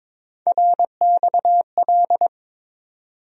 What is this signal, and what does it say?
Morse code, heard as a keyed tone, about 22 wpm, 700 Hz. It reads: RXL
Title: USA Call Sign Postfixes - 3 Letters 22wpm